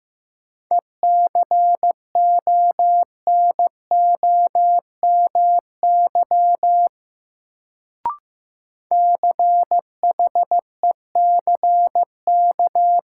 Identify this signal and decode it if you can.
Morse code, heard as a keyed tone, about 15 wpm, 700 Hz. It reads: ECONOMY CHECK
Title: Sentences from Top 1000 Words - Review Part 3 - New Word Drill - 15wpm